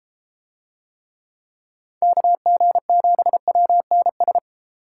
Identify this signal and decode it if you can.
Morse code, heard as a keyed tone, about 31 wpm, 700 Hz. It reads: KG7WNS